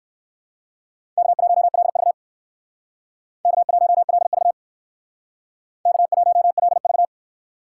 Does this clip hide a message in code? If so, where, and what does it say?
Yes: Morse code from 1.2 to 7.0 s, heard as a keyed tone, about 65 wpm, 700 Hz. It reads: K1LV K1LV K1LV